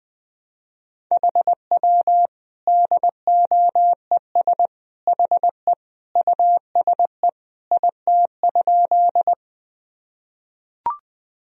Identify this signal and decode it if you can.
Morse code, heard as a keyed tone, about 20 wpm, 700 Hz. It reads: HW DOES HE USE IT?
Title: Sentences from Top 200 Words and Abbreviations 20wpm